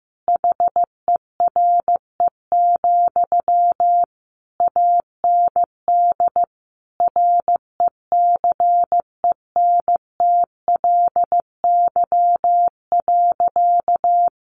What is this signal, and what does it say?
Morse code, heard as a keyed tone, about 15 wpm, 700 Hz. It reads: HERE, AND RECENTLY.
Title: Sentences from Top 1200 Words 15wpm